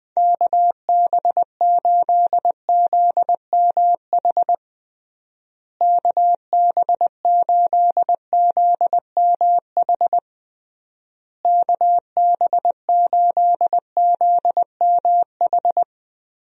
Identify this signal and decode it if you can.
Morse code, heard as a keyed tone, about 21 wpm, 700 Hz. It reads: KB8ZMH KB8ZMH KB8ZMH
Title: USA Call Signs - 6 Characters - Repeated 3x - 20wpm